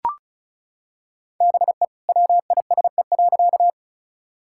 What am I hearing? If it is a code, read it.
Morse code, heard as a keyed tone, about 36 wpm, 700 Hz. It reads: BE WISE.
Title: Sentences from Top 1600 Words 35wpm